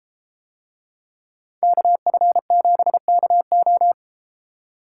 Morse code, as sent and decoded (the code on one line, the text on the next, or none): -.- ..-. --... -.- ---
KF7KO